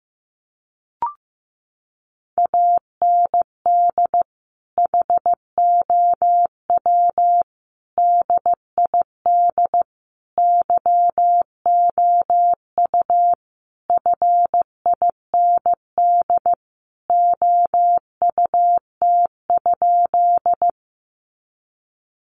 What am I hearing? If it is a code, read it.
Morse code, heard as a keyed tone, about 15 wpm, 700 Hz. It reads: AND HOW DID YOU FIND OUT?
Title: Sentences from Top 100 Words 15wpm